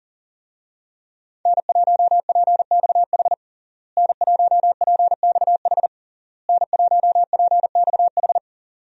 Morse code, heard as a keyed tone, about 40 wpm, 700 Hz. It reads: N1PXH N1PXH N1PXH
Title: USA Call Signs - 5 Characters - Repeated 3x - 40wpm